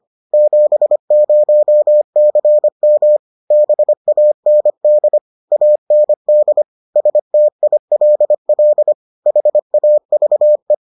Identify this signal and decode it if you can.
Morse code, heard as a keyed tone, about 25 wpm, 600 Hz. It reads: 70CM BAND AND STILL HAVE